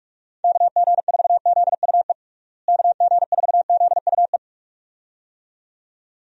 Morse code, heard as a keyed tone, about 41 wpm, 700 Hz. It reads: KG4ZUE KG4ZUE
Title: USA Call Signs - 6 Characters - Repeated 2x - 45wpm